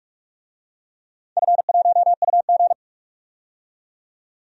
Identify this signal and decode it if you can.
Morse code, heard as a keyed tone, about 43 wpm, 700 Hz. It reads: F1UG
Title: DXCC-100 Call Signs - 4 Characters - 45wpm